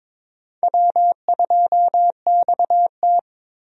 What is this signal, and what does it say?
Morse code, heard as a keyed tone, about 22 wpm, 700 Hz. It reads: W2XT
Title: USA Call Signs - 4 Characters - 22wpm